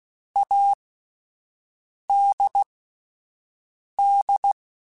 Morse code, heard as a keyed tone, about 16 wpm, spaced as for 5 wpm, 800 Hz. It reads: ADD
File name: Alphabetical / Alphabetical mp3s / As @5